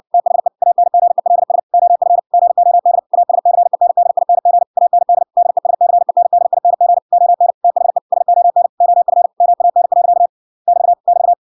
Morse code, heard as a keyed tone, about 56 wpm, 700 Hz. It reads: THE ANGELS OF GOD ASCENDING AND DESCENDING ON THE SON OF MAN. <BT> <BT>